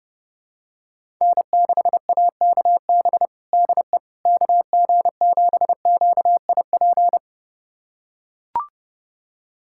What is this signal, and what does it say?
Morse code, heard as a keyed tone, about 30 wpm, 700 Hz. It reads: N6AKB DE KG7QIP
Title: USA Callsign DE Callsign 30wpm